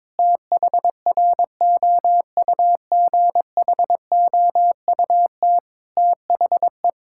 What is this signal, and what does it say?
Morse code, heard as a keyed tone, about 22 wpm, 700 Hz. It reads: THROUGHOUT THE
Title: Sentences from Top 1000 Words - Review Part 5 - 22wpm